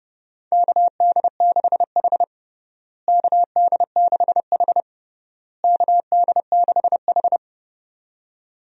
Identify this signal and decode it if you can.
Morse code, heard as a keyed tone, about 32 wpm, 700 Hz. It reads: KD6H KD6H KD6H
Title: USA Call Signs - 4 Characters - Repeated 3x - 30wpm